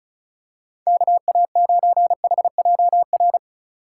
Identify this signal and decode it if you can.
Morse code, heard as a keyed tone, about 36 wpm, 700 Hz. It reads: KA9HJR
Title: USA Call Signs - 6 Characters - 3x Character Spacing - 35wpm